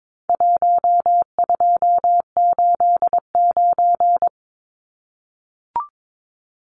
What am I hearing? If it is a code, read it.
Morse code, heard as a keyed tone, about 22 wpm, 700 Hz. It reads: 1289
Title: Numbers - Contest Style - 4-Digit 22wpm